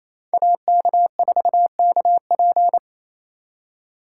Morse code, heard as a keyed tone, about 35 wpm, 700 Hz. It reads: AK4KP